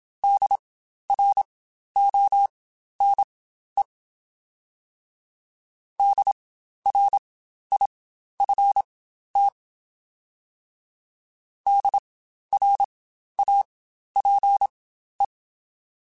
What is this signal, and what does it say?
Morse code, heard as a keyed tone, about 27 wpm, spaced as for 12 wpm, 800 Hz. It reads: DRONE DRIFT DRAPE
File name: Spaced Alphas / Spaced Alphas mp3s / Ds sp @25